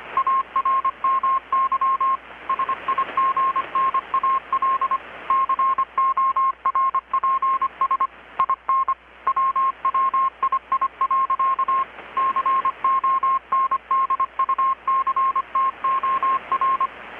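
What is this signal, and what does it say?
Morse code, heard as a keyed tone, about 25 wpm, 1070 Hz. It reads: ARMY SIGNAL CORPS IN WWII. CONDUCTOR
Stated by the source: Morse from KPH's Night of Nights XX maritime radiotelegraphy event, received in Chiapas, Mexico about 3,800 km away on a Kenwood R-600 with a grounded 10 m long-wire antenna.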